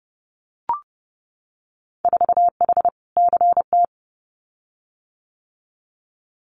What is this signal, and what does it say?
Morse code, heard as a keyed tone, about 30 wpm, 700 Hz. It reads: 4H CT
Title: Winter Field Day Exchanges Encore 30wpm